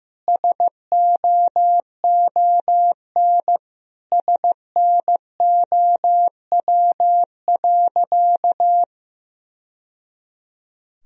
Morse code, heard as a keyed tone, about 15 wpm, 700 Hz. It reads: SOON SNOW.
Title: Sentences from Top 1000 Words - Review Part 3 - 15wpm